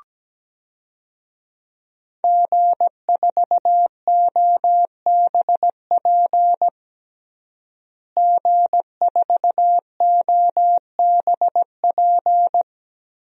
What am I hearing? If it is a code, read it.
Morse code, heard as a keyed tone, about 17 wpm, 700 Hz. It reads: G4OBP G4OBP
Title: DXCC-100 Call Signs - 5 Characters - Repeated 2x - 17wpm